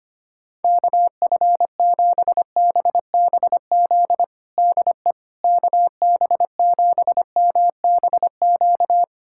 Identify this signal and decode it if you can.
Morse code, heard as a keyed tone, about 25 wpm, 700 Hz. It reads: KF7BBZ DE KB7MBQ
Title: USA Callsign DE Callsign - 6 Characters - 25wpm